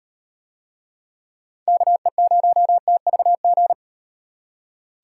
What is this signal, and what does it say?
Morse code, heard as a keyed tone, about 37 wpm, 700 Hz. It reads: KE0TVG